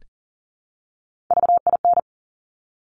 Morse code, heard as a keyed tone, about 42 wpm, 700 Hz. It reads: VIN
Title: Top Three Letter Combinations in Words 40wpm